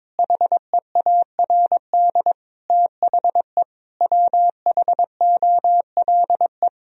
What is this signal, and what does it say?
Morse code, heard as a keyed tone, about 22 wpm, 700 Hz. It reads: HEARD THE WHOLE
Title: Sentences from Top 1000 Words - Review Part 3 - 22wpm